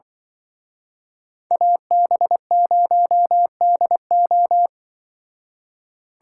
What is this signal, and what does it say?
Morse code, heard as a keyed tone, about 23 wpm, 700 Hz. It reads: AB0DO
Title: Speed Racing - USA 2x2 Call Signs 20wpm